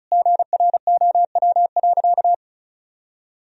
Morse code, heard as a keyed tone, about 35 wpm, 700 Hz. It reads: GROW.
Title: Sentences from Top 1000 Words - Review Part 2 - 35wpm